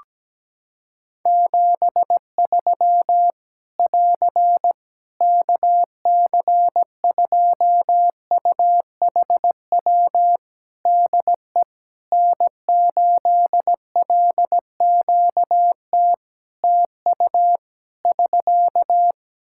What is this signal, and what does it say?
Morse code, heard as a keyed tone, about 17 wpm, 700 Hz. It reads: 73 <AR> KC2UHW DE N8LQT TU <SK>
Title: Practice QSO - Exchange 4 - Ending - 17wpm